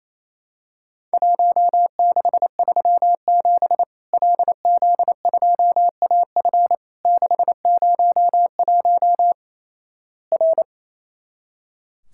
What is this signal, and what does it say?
Morse code, heard as a keyed tone, about 28 wpm, 700 Hz. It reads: 1637 LZ2AF 601 R
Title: Worked All Europe DX Contest - QTC Exchange - 28wpm